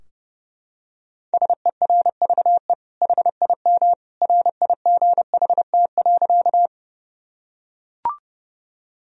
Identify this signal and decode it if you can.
Morse code, heard as a keyed tone, about 30 wpm, 700 Hz. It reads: SERVE HIM RIGHT.
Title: Sentences from Top 1000 Words - Review Part 2 - 30wpm